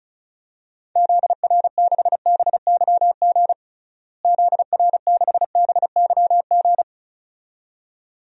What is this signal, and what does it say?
Morse code, heard as a keyed tone, about 36 wpm, 700 Hz. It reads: ZR6BYG ZR6BYG
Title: DXCC-100 Call Signs - 6 Characters - Repeated 2x - 35wpm